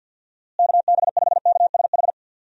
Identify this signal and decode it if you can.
Morse code, heard as a keyed tone, about 50 wpm, 700 Hz. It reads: KB5KSH
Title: USA Call Signs - 6 Characters - Repeated 2x - 50wpm